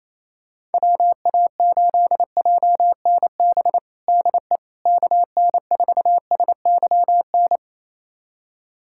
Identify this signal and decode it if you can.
Morse code, heard as a keyed tone, about 28 wpm, 700 Hz. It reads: WA8JNB DE KN4SYN
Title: USA Callsign DE Callsign - 6 Characters - 28wpm